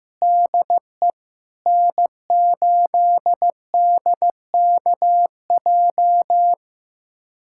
Morse code, heard as a keyed tone, about 15 wpm, 700 Hz. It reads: DE N8DKJ
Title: USA Callsign DE Callsign 15wpm